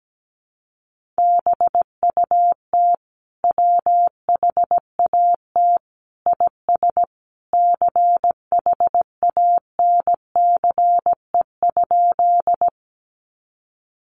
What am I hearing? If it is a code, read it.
Morse code, heard as a keyed tone, about 17 wpm, 700 Hz. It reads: BUT WHAT IS CHANCE?
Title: Sentences from Top 800 Words 17wpm